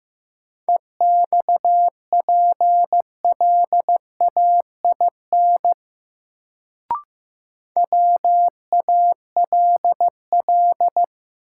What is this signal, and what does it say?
Morse code, heard as a keyed tone, about 15 wpm, 700 Hz. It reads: EXPLAIN WALL
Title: Sentences from Top 1100 Words - New Word Drill Alt - 15wpm